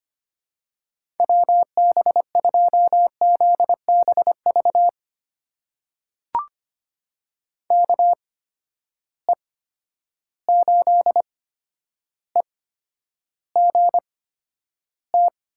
Morse code, heard as a keyed tone, about 25 wpm, spaced as for 11 wpm, 700 Hz. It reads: WB2ZBV KE8EGT
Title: USA Call Signs - 8x Character Spacing - 25wpm